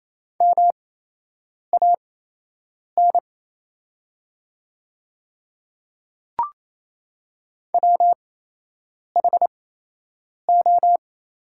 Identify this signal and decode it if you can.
Morse code, heard as a keyed tone, about 28 wpm, spaced as for 8 wpm, 700 Hz. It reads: MAN WHO
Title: 8x Character Spacing - Top 200 Words - 28wpm